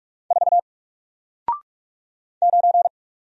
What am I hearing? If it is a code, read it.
Morse code, heard as a keyed tone, about 44 wpm, 700 Hz. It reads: 4 9
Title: Single Letter-Number Alt 45wpm